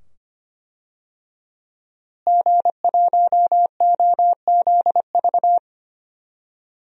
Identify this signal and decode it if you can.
Morse code, heard as a keyed tone, about 25 wpm, 700 Hz. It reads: G1OZV
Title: DXCC-100 Call Signs - 5 Characters - Repeated 2x - 25wpm